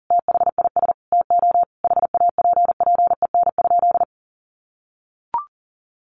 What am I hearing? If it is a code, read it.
Morse code, heard as a keyed tone, about 40 wpm, 700 Hz. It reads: THIS TO HAPPEN?
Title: Sentences from Top 1100 Words 40wpm